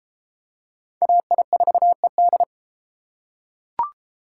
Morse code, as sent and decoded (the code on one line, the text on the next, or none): .- .. ....- . -..
AI4ED